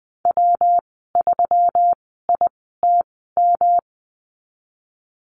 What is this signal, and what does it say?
Morse code, heard as a keyed tone, about 20 wpm, spaced as for 15 wpm, 700 Hz. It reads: W3ITM
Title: 161 Lesson 26g - Letter g - Call Signs 20wpm